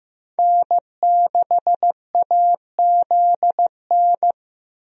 Morse code, heard as a keyed tone, about 15 wpm, 700 Hz. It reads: N6AZN
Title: USA Call Signs - 5 Characters - Repeated 2x - 15wpm